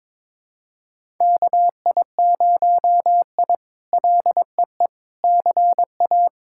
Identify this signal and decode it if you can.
Morse code, heard as a keyed tone, about 22 wpm, 700 Hz. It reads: KI0I LEE CA
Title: K1USN SST QSO Practice - 22wpm